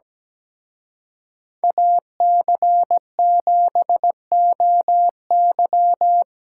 Morse code, heard as a keyed tone, about 20 wpm, 700 Hz. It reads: AC7OY
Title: Speed Racing - USA 2x2 Call Signs 17wpm